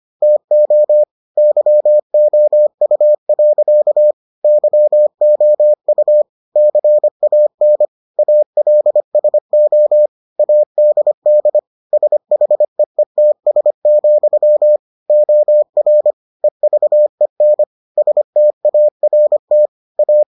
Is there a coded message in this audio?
Yes: Morse code from 0.2 to 20.2 s, heard as a keyed tone, about 25 wpm, 600 Hz. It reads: TO YOU. YOU CAN ALSO ADD SHEETS, OR EVEN START A